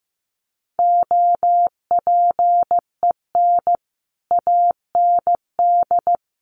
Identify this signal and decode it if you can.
Morse code, heard as a keyed tone, about 15 wpm, 700 Hz. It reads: OPEN AND
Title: Sentences from Top 1400 Words 15wpm